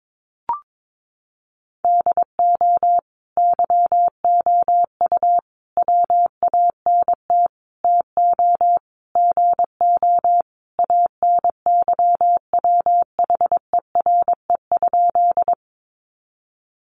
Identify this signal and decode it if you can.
Morse code, heard as a keyed tone, about 22 wpm, 700 Hz. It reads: DO YOU WANT TO GO ANYWHERE?
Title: Sentences from Top 1400 Words 22wpm